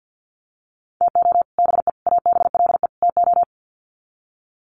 Morse code, heard as a keyed tone, about 50 wpm, 700 Hz. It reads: TO BE ABLE TO
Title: Sets of 4 Words 50wpm